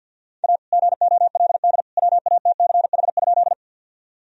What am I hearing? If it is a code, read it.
Morse code, heard as a keyed tone, about 46 wpm, 700 Hz. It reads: A GOLD WATCH?
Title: Sentences from Top 1000 Words - Review Part 3 - 50wpm